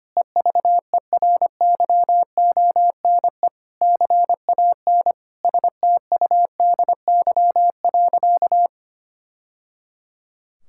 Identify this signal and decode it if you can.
Morse code, heard as a keyed tone, about 25 wpm, 700 Hz. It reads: EVERYONE CAN STUDY.
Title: Sentences from Top 1000 Words - Review Part 5 - 25wpm